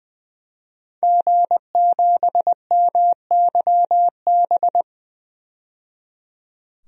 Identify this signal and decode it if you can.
Morse code, heard as a keyed tone, about 20 wpm, 700 Hz. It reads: G7MYB
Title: DXCC-100 Call Signs - 5 Characters - 20wpm